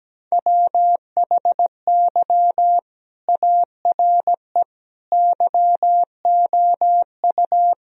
Morse code, heard as a keyed tone, about 17 wpm, 700 Hz. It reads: WHY ARE YOU
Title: Sentences from Top 1000 Words - Review Part 4 - 17wpm